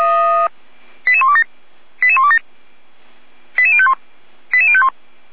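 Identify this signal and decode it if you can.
Five-tone selective call (ZVEI), heard as quick runs of five tones, tones of about 70 ms. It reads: 88127 88127 88051 88051
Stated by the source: ZVEI five-tone call received via SDR.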